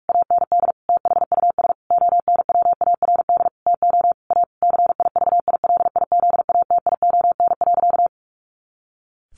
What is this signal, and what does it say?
Morse code, heard as a keyed tone, about 44 wpm, 700 Hz. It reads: AND THUS ONWARD TO A CIVILIZATION.
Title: A Sound Of Thunder 45wpm